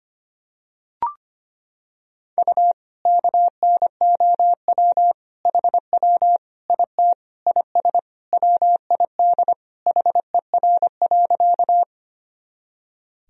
Morse code, heard as a keyed tone, about 25 wpm, 700 Hz. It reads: U KNOW HW IT IS WID HER.
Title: Sentences from Top 100 Words and Abbreviations 25wpm